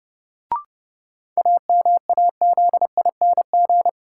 Morse code, heard as a keyed tone, about 30 wpm, 700 Hz. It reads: AMAZING
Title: Sentences from Top 1500 Words - New Word Drill - 30wpm